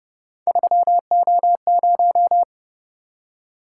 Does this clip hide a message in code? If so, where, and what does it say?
Yes: Morse code from 0.5 to 2.4 s, heard as a keyed tone, about 29 wpm, 700 Hz. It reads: 3O0